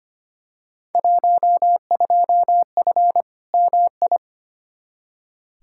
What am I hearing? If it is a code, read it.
Morse code, heard as a keyed tone, about 25 wpm, 700 Hz. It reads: 12F MI
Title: Field Day Exchanges Encore - 25wpm